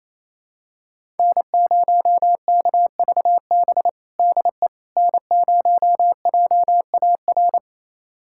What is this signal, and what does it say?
Morse code, heard as a keyed tone, about 28 wpm, 700 Hz. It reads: N0KVB DE N0JAR
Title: USA Callsign DE Callsign - 5 Characters - 28wpm